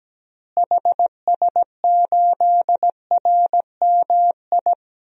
Morse code, heard as a keyed tone, about 17 wpm, 700 Hz. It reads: HS8RMI
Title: DXCC-100 Call Signs - 6 Characters - Repeated 3x - 17wpm